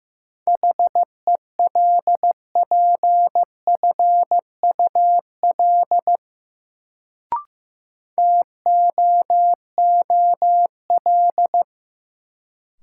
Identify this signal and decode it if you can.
Morse code, heard as a keyed tone, about 15 wpm, 700 Hz. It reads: HELPFUL TOOL
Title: Sentences from Top 1000 Words - Review Part 6 - New Word Drill - 15wpm